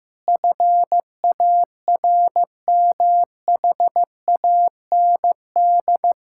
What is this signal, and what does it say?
Morse code, heard as a keyed tone, about 15 wpm, 700 Hz. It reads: FARMHAND
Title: Pangram Challenge v2 - 15wpm